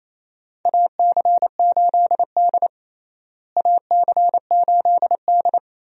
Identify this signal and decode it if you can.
Morse code, heard as a keyed tone, about 27 wpm, 700 Hz. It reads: AC8D AC8D
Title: USA Call Signs - 4 Characters - Repeated 3x - 28wpm